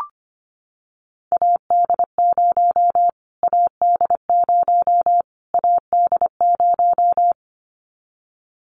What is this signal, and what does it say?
Morse code, heard as a keyed tone, about 25 wpm, 700 Hz. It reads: AD0 AD0 AD0
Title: Callsign Prefixes for Top 25 DXCC - Repeated 3x - 25wpm